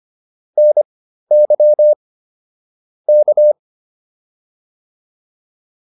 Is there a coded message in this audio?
Yes: Morse code from 0.6 to 3.5 s, heard as a keyed tone, about 25 wpm, spaced as for 13 wpm, 600 Hz. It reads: NY K